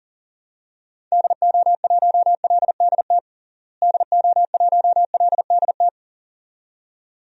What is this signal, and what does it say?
Morse code, heard as a keyed tone, about 40 wpm, 700 Hz. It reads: DO1LDT DO1LDT
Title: DXCC-100 Call Signs - 6 Characters - Repeated 2x - 40wpm